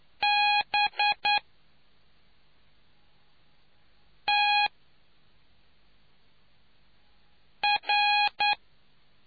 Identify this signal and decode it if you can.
Morse code, heard as a keyed tone, about 9 wpm, spaced as for 3 wpm, 830 Hz. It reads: BTR